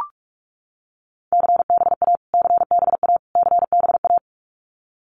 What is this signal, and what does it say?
Morse code, heard as a keyed tone, about 44 wpm, 700 Hz. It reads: CBA CBA CBA
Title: Abbreviations Repeated 3x - 45wpm